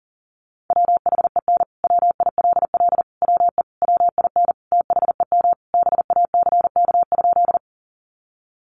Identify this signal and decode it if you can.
Morse code, heard as a keyed tone, about 37 wpm, 700 Hz. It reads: WHEN WILL WE WIN THEM BACK?
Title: Sentences from Top 1500 Words 40wpm